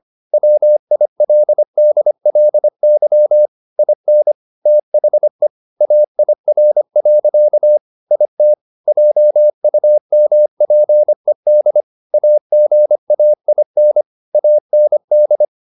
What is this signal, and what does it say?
Morse code, heard as a keyed tone, about 25 wpm, 600 Hz. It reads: WILDLY IN THE AIR. IT JUMPED AGAIN AND